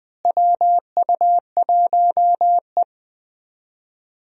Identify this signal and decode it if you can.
Morse code, heard as a keyed tone, about 20 wpm, 700 Hz. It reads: WU1E
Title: USA Call Signs - 4 Characters - Repeated 3x - 20wpm